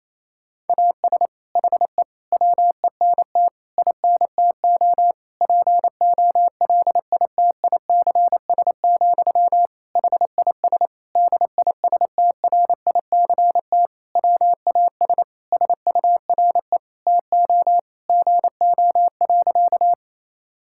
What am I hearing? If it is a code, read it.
Morse code, heard as a keyed tone, about 27 wpm, 700 Hz. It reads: AS HE WENT INTO POLITICS, HIS DISTRICT WAS SURE TO GO.
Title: Sentences from Top 1600 Words 28wpm